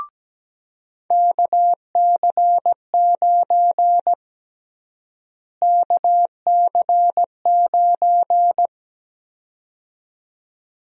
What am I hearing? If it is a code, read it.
Morse code, heard as a keyed tone, about 17 wpm, 700 Hz. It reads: KC9 KC9
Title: USA Call Sign Prefixes - Repeated 2x - 17wpm